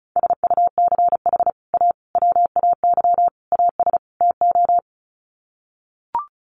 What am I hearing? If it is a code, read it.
Morse code, heard as a keyed tone, about 34 wpm, 700 Hz. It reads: SUCH A WAY AS TO
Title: Sets of 5 Words Encore2 35wpm